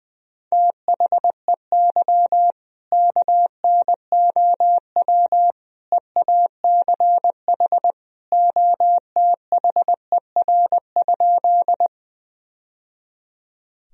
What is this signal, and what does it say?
Morse code, heard as a keyed tone, about 20 wpm, 700 Hz. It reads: THEY KNOW EACH OTHER?
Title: Sentences from Top 1000 Words - Review Part 1 - 20wpm